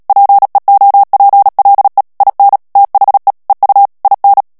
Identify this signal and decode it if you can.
Morse code, heard as a keyed tone, about 37 wpm, 800 Hz. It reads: PEOPLE IN THE EU IN
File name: morse mp3 / ANGLAIS texte clair 2 vitesse 35 mots minute